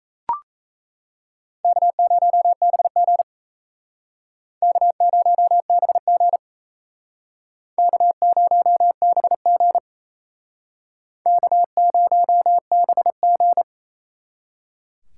Morse code, heard as a keyed tone, about 36 wpm, 700 Hz. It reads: K0BG K0BG K0BG K0BG